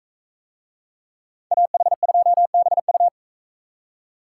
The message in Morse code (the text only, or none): AH2BU